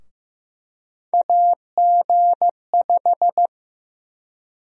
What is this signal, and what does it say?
Morse code, heard as a keyed tone, about 16 wpm, 700 Hz. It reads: AG5
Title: USA Call Sign Prefixes - Repeated 2x - 15wpm